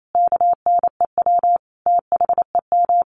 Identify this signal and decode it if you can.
Morse code, heard as a keyed tone, about 28 wpm, 700 Hz. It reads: KNEW THEM
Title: Sentences from Top 1400 Words 28wpm